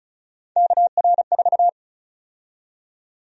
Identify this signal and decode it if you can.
Morse code, heard as a keyed tone, about 36 wpm, 700 Hz. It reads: KR4
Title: USA Call Sign Prefixes - Repeated 2x - 35wpm